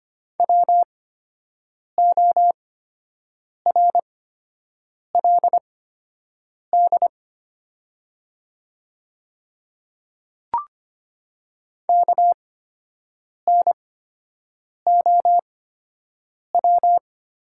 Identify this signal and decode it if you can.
Morse code, heard as a keyed tone, about 25 wpm, spaced as for 7 wpm, 700 Hz. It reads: WORLD KNOW
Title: 8x Character Spacing - Top 200 Words - 25wpm